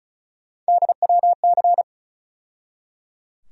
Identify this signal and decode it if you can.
Morse code, heard as a keyed tone, about 34 wpm, 700 Hz. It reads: DWC
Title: USA Call Sign Postfixes - 3 Letters 35wpm